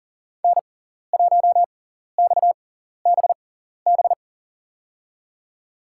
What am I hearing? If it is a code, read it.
Morse code, heard as a keyed tone, about 40 wpm, spaced as for 14 wpm, 700 Hz. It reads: N1XBB
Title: USA Call Signs - 5 Characters - 6x Character Spacing - 40wpm